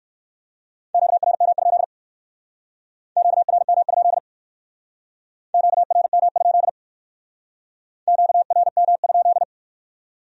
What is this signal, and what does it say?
Morse code, heard as a keyed tone, about 59 wpm, 700 Hz. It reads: QRM? QRM? QRM? QRM?